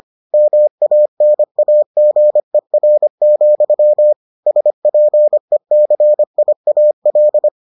Morse code, heard as a keyed tone, about 25 wpm, 600 Hz. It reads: MANAGER, SPECIAL